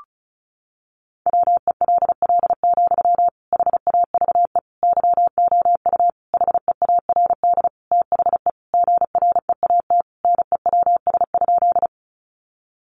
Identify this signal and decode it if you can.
Morse code, heard as a keyed tone, about 34 wpm, 700 Hz. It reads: WELL, HAVE YOU HEARD THE GREAT NEWS?
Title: Sentences from Top 800 Words 35wpm